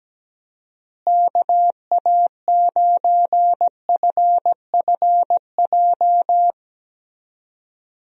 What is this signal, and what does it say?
Morse code, heard as a keyed tone, about 17 wpm, 700 Hz. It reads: KA9FFJ
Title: USA Call Signs - 6 Characters - Repeated 3x - 17wpm